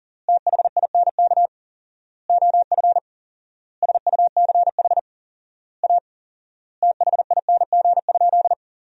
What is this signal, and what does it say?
Morse code, heard as a keyed tone, about 40 wpm, 700 Hz. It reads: THINK OF SUCH A THING?
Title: 4x Word-Spacing - Sentences from Top 200 Words - 40wpm